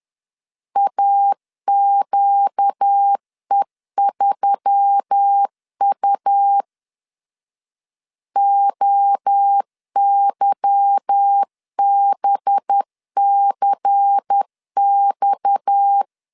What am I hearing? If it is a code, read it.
Morse code, heard as a keyed tone, about 10 wpm, 800 Hz. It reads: AQE3U OYBCX
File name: LEZIONI / 8 YJ45 / 8 Y 10wpm 2615